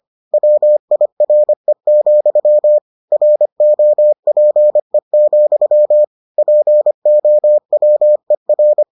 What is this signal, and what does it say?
Morse code, heard as a keyed tone, about 25 wpm, 600 Hz. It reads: WIRE, ROPE, POWER